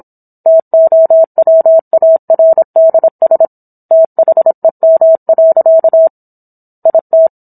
Morse code, heard as a keyed tone, about 27 wpm, 650 Hz. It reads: TOWARDS THEM. IT
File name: Book One / 17-Chapter 17 - 26 WPM